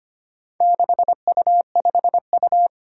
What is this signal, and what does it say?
Morse code, heard as a keyed tone, about 25 wpm, 700 Hz. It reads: 6U5U
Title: Four Challenging Letter-Number Combinations 25wpm